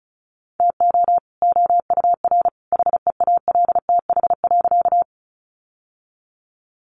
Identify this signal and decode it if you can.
Morse code, heard as a keyed tone, about 34 wpm, 700 Hz. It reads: TO OUR HEALTH.
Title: Sentences from Top 600 Words 35wpm